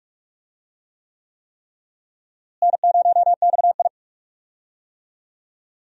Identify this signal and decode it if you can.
Morse code, heard as a keyed tone, about 45 wpm, 700 Hz. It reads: N0XI
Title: USA Call Signs - 4 Characters - Repeated 3x - 45wpm